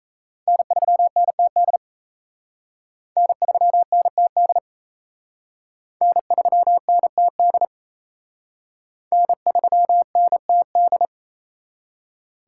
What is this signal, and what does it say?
Morse code, heard as a keyed tone, about 34 wpm, 700 Hz. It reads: N3NTD N3NTD N3NTD N3NTD
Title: Speed Racing - USA 1x3 Call Signs 28wpm